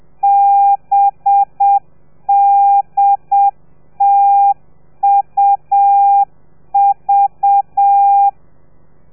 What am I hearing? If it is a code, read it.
Morse code, heard as a keyed tone, about 7 wpm, 800 Hz. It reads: BDTUV
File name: telegrafia / i2tfj / Lez4 30